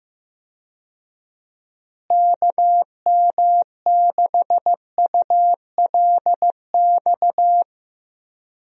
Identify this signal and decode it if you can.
Morse code, heard as a keyed tone, about 15 wpm, 700 Hz. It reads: KM6ULX
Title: USA Call Signs - 6 Characters - Repeated 2x - 15wpm